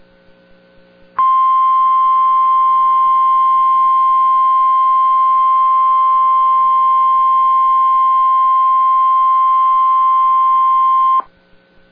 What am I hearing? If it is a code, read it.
Emergency Alert System (SAME) tones (the 1050 Hz attention tone). It is heard from 1.2 to 11.2 s.